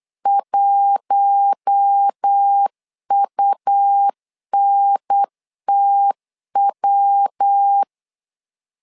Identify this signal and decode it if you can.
Morse code, heard as a keyed tone, about 8 wpm, 800 Hz. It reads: 1UNTW
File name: LEZIONI / 4 Q1 8wpm 3035